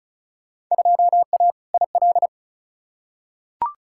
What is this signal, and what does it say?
Morse code, heard as a keyed tone, about 36 wpm, 700 Hz. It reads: 2A IL
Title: Field Day Exchanges - 35wpm